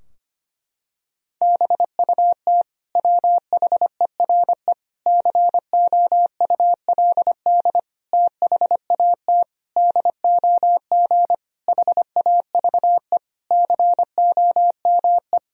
Morse code, heard as a keyed tone, about 25 wpm, 700 Hz. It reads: BUT WHERE COULD THAT DOG HAVE COME
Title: Sentences from Top 1000 Words - Review Part 2 - 25wpm